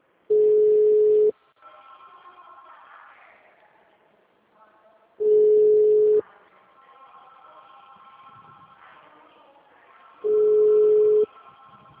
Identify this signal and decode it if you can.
Ringback tone (Europe, 425 Hz), with speech.